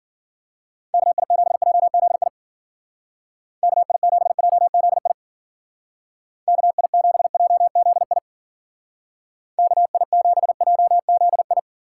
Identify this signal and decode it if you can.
Morse code, heard as a keyed tone, about 48 wpm, 700 Hz. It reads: KI7JZI KI7JZI KI7JZI KI7JZI
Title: Speed Racing - USA 2x3 Call Signs 40wpm